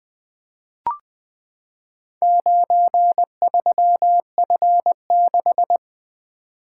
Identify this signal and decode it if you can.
Morse code, heard as a keyed tone, about 20 wpm, 700 Hz. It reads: 93F6
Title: Four Letter-Number Combinations - 20wpm